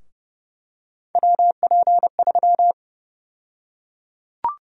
Morse code, heard as a keyed tone, about 31 wpm, 700 Hz. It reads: WP3
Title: USA Call Sign Prefixes - 30wpm